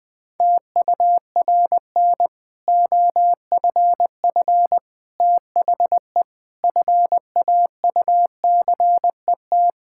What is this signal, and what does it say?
Morse code, heard as a keyed tone, about 20 wpm, 700 Hz. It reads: TURN OFF THE FAUCET
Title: Difficult Sentences with Hints - 20wpm